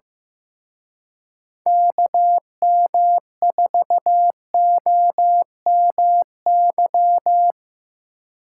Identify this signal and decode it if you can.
Morse code, heard as a keyed tone, about 17 wpm, 700 Hz. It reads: KM4OMY
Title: Speed Racing - USA 2x3 Call Signs 15wpm